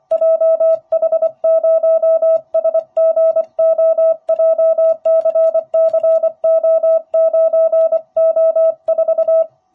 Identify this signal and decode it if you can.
Morse code, heard as a keyed tone, about 25 wpm, 650 Hz. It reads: JH0SGO JCCO9O4